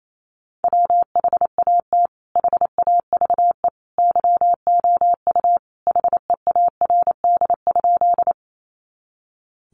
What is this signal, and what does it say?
Morse code, heard as a keyed tone, about 28 wpm, 700 Hz. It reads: WHAT HAVE YOU HEARD?
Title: Sentences from Top 400 Words 28wpm